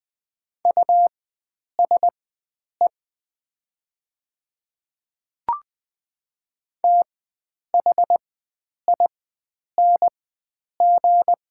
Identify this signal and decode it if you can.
Morse code, heard as a keyed tone, about 20 wpm, spaced as for 9 wpm, 700 Hz. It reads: USE THING
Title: 4x Character Spacing - Top 100 Words - 20wpm